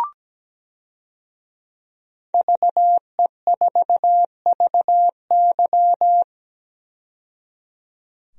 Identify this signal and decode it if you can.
Morse code, heard as a keyed tone, about 17 wpm, 700 Hz. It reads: VE4VY
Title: DXCC-100 Call Signs - 5 Characters - 17wpm